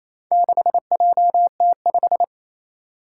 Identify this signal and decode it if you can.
Morse code, heard as a keyed tone, about 28 wpm, 700 Hz. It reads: 6JT5
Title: Four Challenging Letter-Number Combinations 28wpm